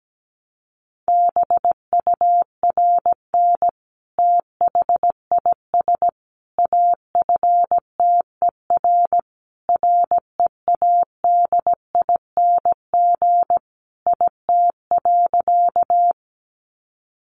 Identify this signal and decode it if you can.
Morse code, heard as a keyed tone, about 17 wpm, 700 Hz. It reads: BURN THIS AFTER READING IT.
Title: Sentences from Top 1300 Words 17wpm